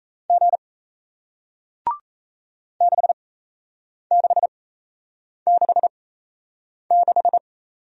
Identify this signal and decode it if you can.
Morse code, heard as a keyed tone, about 38 wpm, spaced as for 8 wpm, 700 Hz. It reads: G 6666